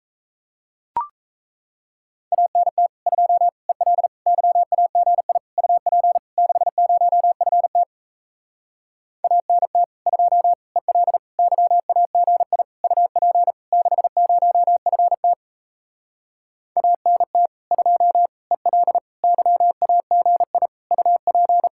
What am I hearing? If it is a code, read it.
Morse code, heard as a keyed tone, about 37 wpm, 700 Hz. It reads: ANT 2 EL YAGI UP 60FT ANT 2 EL YAGI UP 60FT ANT 2 EL YAGI UP